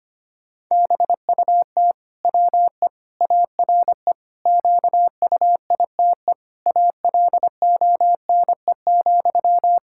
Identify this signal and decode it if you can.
Morse code, heard as a keyed tone, about 25 wpm, 700 Hz. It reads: BUT WE ARE QUITE ALONE,
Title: Sentences from Top 1600 Words 25wpm